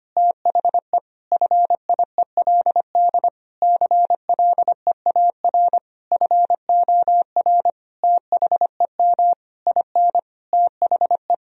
Morse code, heard as a keyed tone, about 25 wpm, 700 Hz. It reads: THE FIELD CLEAR FOR THEM IN THE
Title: Sentences from Top 1000 Words - Review Part 2 - 25wpm